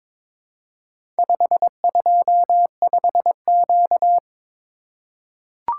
Morse code, heard as a keyed tone, about 22 wpm, 700 Hz. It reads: 525Q
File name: practice / 22wpm / Four Letter-Number Combinations - 22wpm